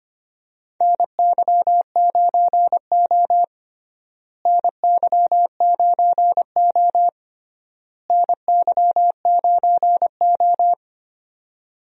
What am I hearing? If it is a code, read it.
Morse code, heard as a keyed tone, about 25 wpm, 700 Hz. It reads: NY9O NY9O NY9O
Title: USA Call Signs - 4 Characters - Repeated 3x - 25wpm